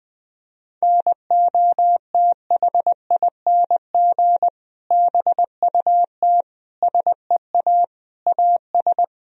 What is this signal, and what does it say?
Morse code, heard as a keyed tone, about 20 wpm, 700 Hz. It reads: NOTHING BUT SEA AS
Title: Sentences from Top 1000 Words - Review Part 1 - 20wpm